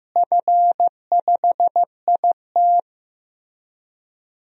Morse code, heard as a keyed tone, about 15 wpm, 700 Hz. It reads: F5IT
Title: DXCC-100 Call Signs - 4 Characters - Repeated 2x - 15wpm